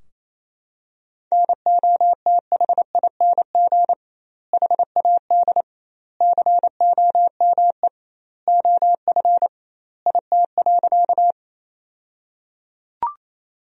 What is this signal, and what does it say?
Morse code, heard as a keyed tone, about 28 wpm, 700 Hz. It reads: NOTHING HAD COME OF IT.
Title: 2x Word-Spacing - Sentences from Top 500 Words - 28wpm